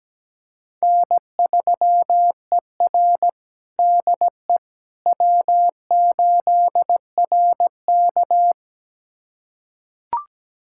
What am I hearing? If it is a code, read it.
Morse code, heard as a keyed tone, about 17 wpm, 700 Hz. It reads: N3ER DE W8RK
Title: USA Callsign DE Callsign - 4 Characters - 17wpm